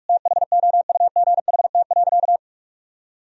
Morse code, heard as a keyed tone, about 46 wpm, 700 Hz. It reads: THOUGHT.
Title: Sentences from Top 1000 Words - Review Part 3 - 45wpm